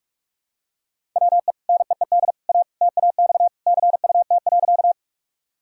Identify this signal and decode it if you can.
Morse code, heard as a keyed tone, about 46 wpm, 700 Hz. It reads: WE NEED A TAX CUT.
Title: Sentences from Top 1000 Words - Review Part 5 - 45wpm